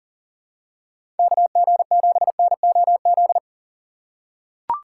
Morse code, heard as a keyed tone, about 40 wpm, 700 Hz. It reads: KG7NOZ
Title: USA Call Signs - 6 Characters - 6x Character Spacing - 40wpm